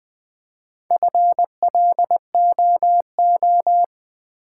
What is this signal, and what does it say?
Morse code, heard as a keyed tone, about 20 wpm, 700 Hz. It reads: FLOO
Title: Build A Word - 1 Letter At A Time - 5-Letter Words - 20wpm